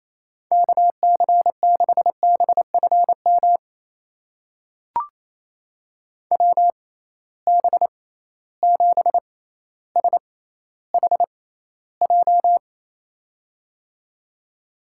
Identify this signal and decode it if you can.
Morse code, heard as a keyed tone, about 28 wpm, spaced as for 14 wpm, 700 Hz. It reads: KC6BFM WB7SHJ
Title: USA Call Signs - 6 Characters - 6x Character Spacing - 28wpm